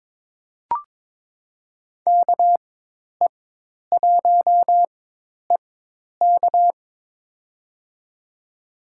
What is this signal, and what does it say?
Morse code, heard as a keyed tone, about 21 wpm, spaced as for 10 wpm, 700 Hz. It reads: KE1EK
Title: USA Call Signs - 5 Characters - 4x Character Spacing - 22wpm